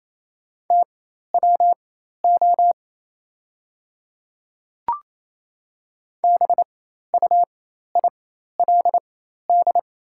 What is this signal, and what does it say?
Morse code, heard as a keyed tone, about 28 wpm, spaced as for 13 wpm, 700 Hz. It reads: TWO BUILD
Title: 4x Character Spacing - Top 200 Words - 28wpm